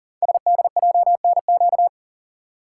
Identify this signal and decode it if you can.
Morse code, heard as a keyed tone, about 39 wpm, 700 Hz. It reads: SDJNQ
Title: Code Groups - 5 Letter:Numbers - 40wpm